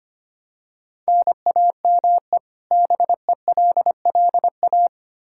Morse code, heard as a keyed tone, about 25 wpm, 700 Hz. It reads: NAME BELLA
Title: Practice QSO - Exchange 1 - Beginning - 25wpm